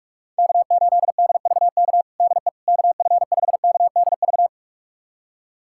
Morse code, heard as a keyed tone, about 46 wpm, 700 Hz. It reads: K8DVK DE KF5KDV
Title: USA Callsign DE Callsign 45wpm